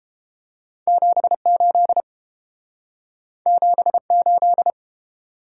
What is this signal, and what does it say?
Morse code, heard as a keyed tone, about 30 wpm, 700 Hz. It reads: 78 78